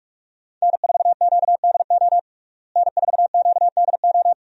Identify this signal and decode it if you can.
Morse code, heard as a keyed tone, about 46 wpm, 700 Hz. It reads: N4QDO N4QDO
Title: USA Call Signs - 5 Characters - Repeated 2x - 45wpm